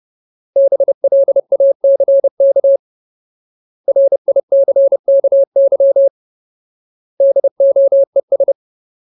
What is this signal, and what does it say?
Morse code, heard as a keyed tone, about 30 wpm, 550 Hz. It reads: BLACK RICKY DOES
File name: Ultimate Word List / 30WPM Ultimate